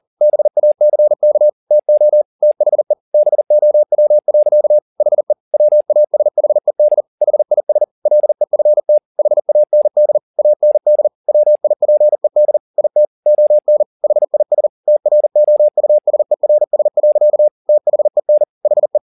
Morse code, heard as a keyed tone, about 40 wpm, 600 Hz. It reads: BACK TO THE BOW. HE WASHED HIS LEFT HAND AND WIPED IT ON HIS TROUSERS. THEN HE
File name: the old man and the sea 2 40wpm